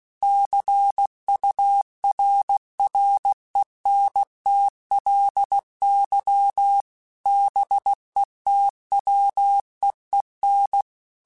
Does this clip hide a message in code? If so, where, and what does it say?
Yes: Morse code from 0.2 to 10.8 s, heard as a keyed tone, about 16 wpm, 800 Hz. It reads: CURRENTLY BETWEEN